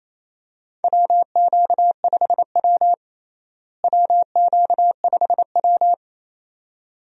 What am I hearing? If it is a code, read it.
Morse code, heard as a keyed tone, about 28 wpm, 700 Hz. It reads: WQ5W WQ5W
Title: USA Call Signs - 4 Characters - Repeated 2x - 28wpm